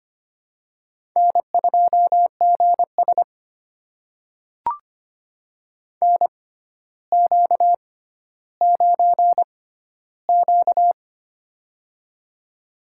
Morse code, heard as a keyed tone, about 25 wpm, spaced as for 13 wpm, 700 Hz. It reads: N2GS NQ9Q
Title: USA Call Signs - 4 Characters - 6x Character Spacing - 25wpm